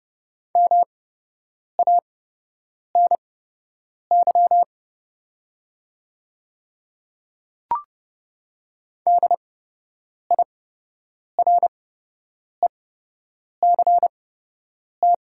Morse code, heard as a keyed tone, about 30 wpm, spaced as for 8 wpm, 700 Hz. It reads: MANY DIRECT
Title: 8x Character Spacing - Top 500 Words - 30wpm